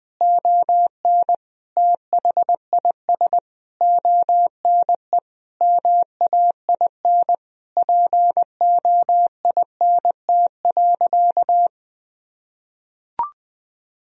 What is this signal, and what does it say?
Morse code, heard as a keyed tone, about 20 wpm, 700 Hz. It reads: ON THIS ONE MAIN POINT.
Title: Sentences from Top 1000 Words - Review Part 2 - 20wpm